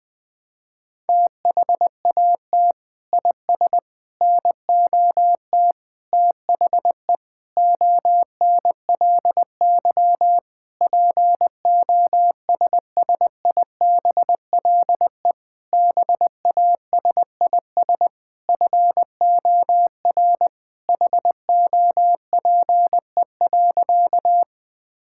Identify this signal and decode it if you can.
Morse code, heard as a keyed tone, about 21 wpm, 700 Hz. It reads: THAT IS NOT THE ONLY POSSIBLE BASIS FOR HOPE.
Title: Sentences from Top 1000 Words - Review Part 5 - 20wpm